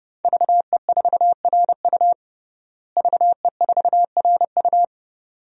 Morse code, heard as a keyed tone, about 30 wpm, 700 Hz. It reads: VE4RU VE4RU
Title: DXCC-100 Call Signs - 5 Characters - Repeated 2x - 30wpm